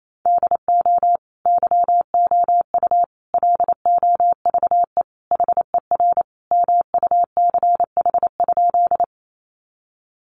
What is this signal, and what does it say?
Morse code, heard as a keyed tone, about 28 wpm, 700 Hz. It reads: DO YOU LOVE HER MUCH?
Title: Sentences from Top 400 Words 28wpm